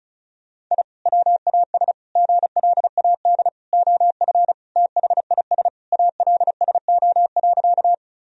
Morse code, heard as a keyed tone, about 34 wpm, 700 Hz. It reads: I WAS GLAD OF THIS ALSO.
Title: Sentences from Top 1000 Words - Review Part 1 - 35wpm